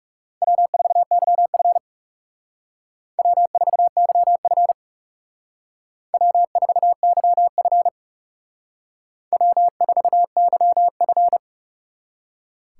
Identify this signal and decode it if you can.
Morse code, heard as a keyed tone, about 35 wpm, 700 Hz. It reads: W4YF W4YF W4YF W4YF
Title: Speed Racing - USA 1x2 Call Signs 30wpm